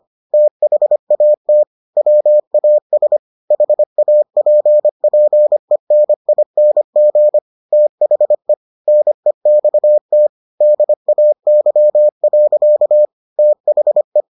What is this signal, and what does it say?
Morse code, heard as a keyed tone, about 25 wpm, 600 Hz. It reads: THAT WAS HAPPENING THE NEXT DAY. THE